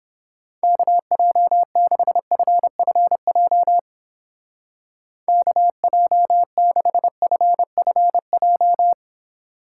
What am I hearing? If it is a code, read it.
Morse code, heard as a keyed tone, about 29 wpm, 700 Hz. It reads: KJ6FFJ KJ6FFJ